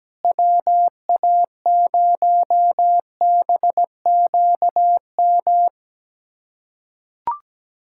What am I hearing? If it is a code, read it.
Morse code, heard as a keyed tone, about 17 wpm, 700 Hz. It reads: WA0BQM
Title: USA Callsign DE Callsign - 6 Characters - 17wpm